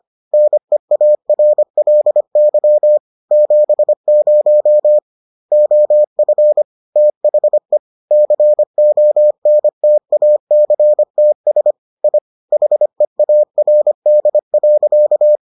Morse code, heard as a keyed tone, about 25 wpm, 600 Hz. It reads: NEARLY 70 OF THE CONTACTS I HEARD.